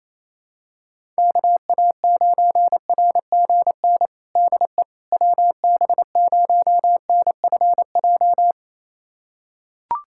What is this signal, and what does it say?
Morse code, heard as a keyed tone, about 28 wpm, 700 Hz. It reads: KA9RGN DE WB0NFJ
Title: USA Callsign DE Callsign - 6 Characters - 28wpm